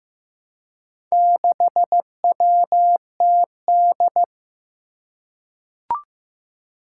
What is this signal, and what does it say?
Morse code, heard as a keyed tone, about 15 wpm, 700 Hz. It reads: 6WTD
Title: Four Challenging Letter-Number Combinations 15wpm